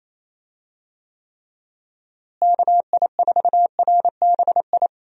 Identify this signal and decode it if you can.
Morse code, heard as a keyed tone, about 28 wpm, 700 Hz. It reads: KI4RBI
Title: USA Call Signs - 6 Characters - Repeated 2x - 28wpm